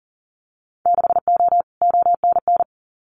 Morse code, heard as a keyed tone, about 40 wpm, 700 Hz. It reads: 6O ONN
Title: Winter Field Day Exchanges Encore 40wpm